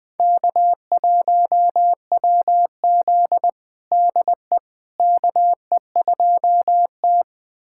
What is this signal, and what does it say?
Morse code, heard as a keyed tone, about 20 wpm, 700 Hz. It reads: K1WZ DE KE2T
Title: USA Callsign DE Callsign - 4 Characters - 20wpm